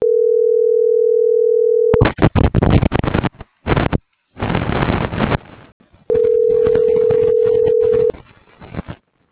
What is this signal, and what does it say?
Ringback tone (North America), 440 + 480 Hz.